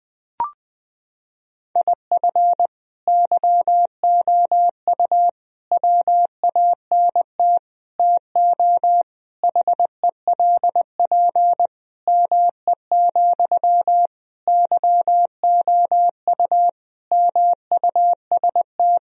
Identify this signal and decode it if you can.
Morse code, heard as a keyed tone, about 20 wpm, 700 Hz. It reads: IF YOU WANT TO HELP ME, YOU MUST
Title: Sentences from Top 1000 Words - Review Part 1 - 20wpm